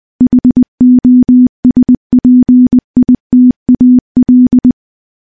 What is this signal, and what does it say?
Morse code, heard as a keyed tone, about 21 wpm, 250 Hz. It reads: HOSPITAL
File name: morse / morse-training-20WPM